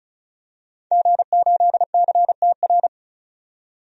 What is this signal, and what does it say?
Morse code, heard as a keyed tone, about 36 wpm, 700 Hz. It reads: G8CTR
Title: DXCC-100 Call Signs - 5 Characters - 35wpm